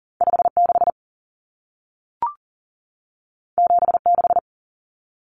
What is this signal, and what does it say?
Morse code, heard as a keyed tone, about 40 wpm, 700 Hz. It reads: 56 76
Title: Common Cut Numbers - 2-Digit 40wpm